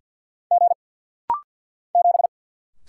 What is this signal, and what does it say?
Morse code, heard as a keyed tone, about 49 wpm, 700 Hz. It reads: G 7
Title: Single Letter-Number Alt - Rapid-Fire 50wpm